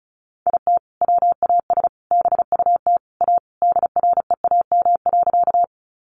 Morse code, heard as a keyed tone, about 34 wpm, 700 Hz. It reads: IT WAS BUT A DREAM.
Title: Sentences from Top 1500 Words 35wpm